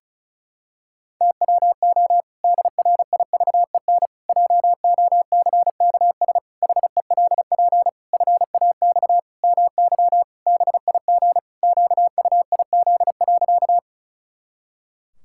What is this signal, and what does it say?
Morse code, heard as a keyed tone, about 34 wpm, 700 Hz. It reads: TWO DRIVEN JOCKS HELP FAX MY BIG QUIZ.
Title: Pangram Challenge v2 - 35wpm